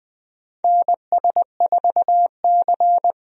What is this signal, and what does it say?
Morse code, heard as a keyed tone, about 21 wpm, 700 Hz. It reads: NS4C
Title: USA Call Signs - 4 Characters - 20wpm